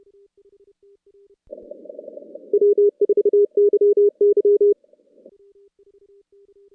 Morse code, heard as a keyed tone, about 30 wpm, 410 Hz. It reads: W4YY
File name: contests / CW-SS-k5tr- -2100